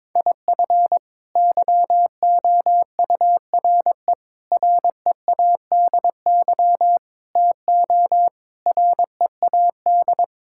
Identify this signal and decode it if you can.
Morse code, heard as a keyed tone, about 22 wpm, 700 Hz. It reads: IF YOURE READY TO READ
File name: practice / 22wpm / Sets of 6 Words - 22wpm